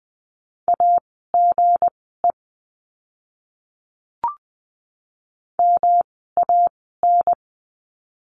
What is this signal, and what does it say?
Morse code, heard as a keyed tone, about 20 wpm, spaced as for 14 wpm, 700 Hz. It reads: AGE MAN
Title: 160 Lesson 26f - Letter g - Words 20wpm